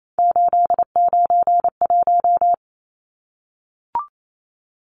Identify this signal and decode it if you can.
Morse code, heard as a keyed tone, about 28 wpm, 700 Hz. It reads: 891
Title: Numbers - 3-Digit 28wpm